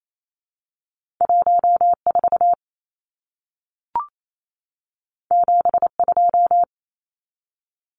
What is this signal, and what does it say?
Morse code, heard as a keyed tone, about 28 wpm, 700 Hz. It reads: 14 72
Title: Numbers - 2-Digit 28wpm